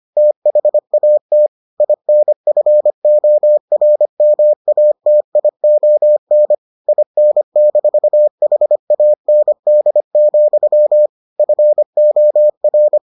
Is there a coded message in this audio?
Yes: Morse code from 0.2 to 13.0 s, heard as a keyed tone, about 25 wpm, 600 Hz. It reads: THAT INFORMATION IN-HAND, FOR